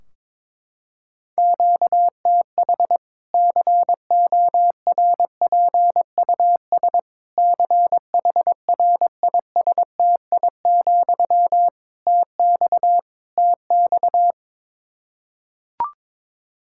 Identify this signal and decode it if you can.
Morse code, heard as a keyed tone, about 22 wpm, 700 Hz. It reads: QTH CORPUS CHRISTI, TX TX
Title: Practice QSO - Exchange 1 - Beginning - 22wpm